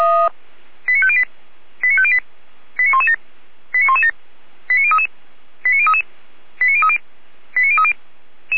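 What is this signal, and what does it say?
Five-tone selective call (ZVEI), heard as quick runs of five tones, tones of about 80 ms. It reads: 89508 89508 89117 89117 89033 89033 89030 89030
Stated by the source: ZVEI five-tone selective call 89085 received with an SDR.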